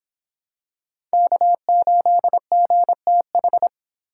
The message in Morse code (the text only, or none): K8GTH